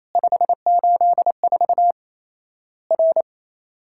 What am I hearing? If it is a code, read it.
Morse code, heard as a keyed tone, about 28 wpm, 700 Hz. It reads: 584 R
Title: Worked All Europe DX Contest - QTC Exchange - 28wpm